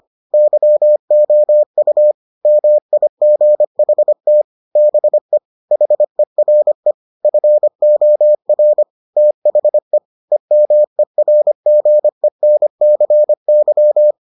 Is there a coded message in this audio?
Yes: Morse code from 0.3 to 14.1 s, heard as a keyed tone, about 25 wpm, 600 Hz. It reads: YOU MIGHT BE HERE FOR THE EMERGENCY